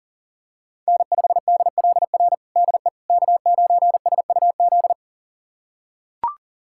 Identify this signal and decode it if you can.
Morse code, heard as a keyed tone, about 40 wpm, 700 Hz. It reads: N5DLR DE K9SUZ
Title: USA Callsign DE Callsign - 5 Characters - 40wpm